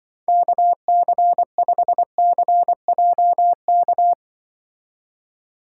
Morse code, heard as a keyed tone, about 24 wpm, 700 Hz. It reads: KC5CJK
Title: Speed Racing - USA 2x3 Call Signs 20wpm